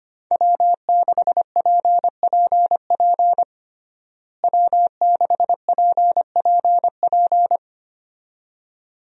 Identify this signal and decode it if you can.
Morse code, heard as a keyed tone, about 25 wpm, 700 Hz. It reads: W6PPP W6PPP
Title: USA Call Signs - 5 Characters - Repeated 2x - 25wpm